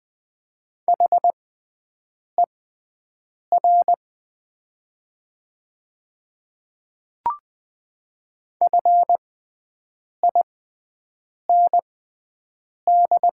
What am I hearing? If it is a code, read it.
Morse code, heard as a keyed tone, about 20 wpm, spaced as for 7 wpm, 700 Hz. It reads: HER FIND
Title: 6x Character Spacing - Top 500 Words - 20wpm